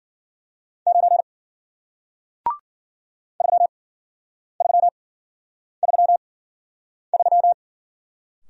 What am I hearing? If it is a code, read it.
Morse code, heard as a keyed tone, about 54 wpm, spaced as for 9 wpm, 700 Hz. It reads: 9 3333